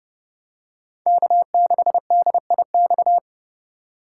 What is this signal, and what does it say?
Morse code, heard as a keyed tone, about 30 wpm, 700 Hz. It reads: K6DIX
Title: USA Call Signs - 5 Characters - 3x Character Spacing - 30wpm